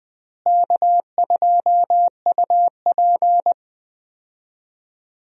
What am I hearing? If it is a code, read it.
Morse code, heard as a keyed tone, about 20 wpm, 700 Hz. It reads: K2UP
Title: USA Call Signs - 4 Characters - Repeated 3x - 20wpm